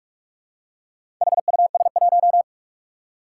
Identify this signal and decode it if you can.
Morse code, heard as a keyed tone, about 46 wpm, 700 Hz. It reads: HUS1
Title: Four Challenging Letter-Number Combinations 45wpm